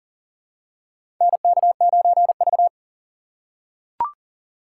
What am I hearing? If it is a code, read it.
Morse code, heard as a keyed tone, about 37 wpm, 700 Hz. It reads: NK9V
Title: USA Call Signs - 4 Characters - 3x Character Spacing - 40wpm